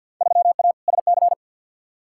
Morse code, heard as a keyed tone, about 50 wpm, 700 Hz. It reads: 3A SC
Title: Field Day Exchanges Encore - 50wpm